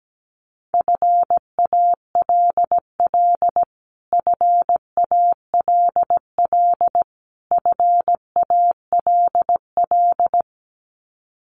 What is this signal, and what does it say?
Morse code, heard as a keyed tone, about 17 wpm, 700 Hz. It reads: FALL FALL FALL
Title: Top 500 Words Repeated 17wpm